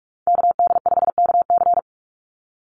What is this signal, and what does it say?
Morse code, heard as a keyed tone, about 44 wpm, 700 Hz. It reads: KD5KC
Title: NA SOTA Activator Callsigns 45wpm